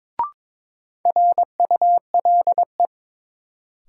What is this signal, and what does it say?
Morse code, heard as a keyed tone, about 23 wpm, 700 Hz. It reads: RULE
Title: Sentences from Top 400 Words - New Word Drill - 22wpm